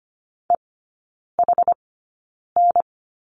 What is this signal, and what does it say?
Morse code, heard as a keyed tone, about 25 wpm, spaced as for 9 wpm, 700 Hz. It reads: EHN
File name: Lesson 14e - Number 2 - Three Characters 25wpm